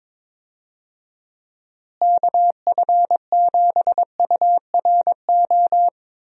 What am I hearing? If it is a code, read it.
Morse code, heard as a keyed tone, about 22 wpm, 700 Hz. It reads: KF7URO
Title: USA Call Signs - 6 Characters - Repeated 2x - 22wpm